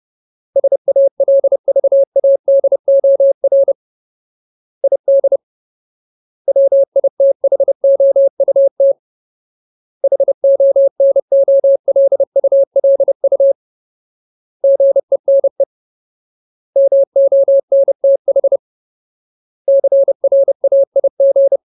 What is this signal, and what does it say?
Morse code, heard as a keyed tone, about 30 wpm, 550 Hz. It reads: SALVADOR ID WITHOUT HONOLULU GENE MONTH CRAIG
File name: Ultimate Word List / 30WPM Ultimate